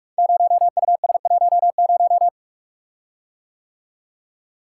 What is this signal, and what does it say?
Morse code, heard as a keyed tone, about 44 wpm, 700 Hz. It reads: 0US10
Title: Code Groups - 5 Letter:Numbers - 45wpm